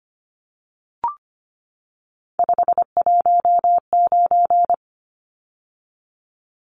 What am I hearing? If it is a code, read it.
Morse code, heard as a keyed tone, about 25 wpm, 700 Hz. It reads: 519
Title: Numbers - 3-Digit 25wpm